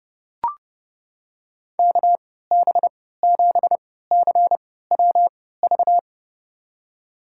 Morse code, heard as a keyed tone, about 30 wpm, spaced as for 17 wpm, 700 Hz. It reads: KB7CWV
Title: USA Call Signs - 6 Characters - 3x Character Spacing - 30wpm